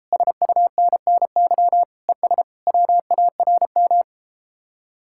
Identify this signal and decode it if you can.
Morse code, heard as a keyed tone, about 33 wpm, 700 Hz. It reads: SUNNY ES WARM